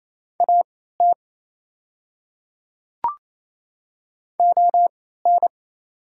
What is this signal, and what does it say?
Morse code, heard as a keyed tone, about 27 wpm, spaced as for 16 wpm, 690 Hz. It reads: AT ON
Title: 3x Character Spacing - Top 100 Words - 28wpm